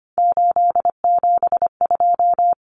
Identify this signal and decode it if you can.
Morse code, heard as a keyed tone, about 25 wpm, 700 Hz. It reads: 872
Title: Common Cut Numbers - 3-Digit 25wpm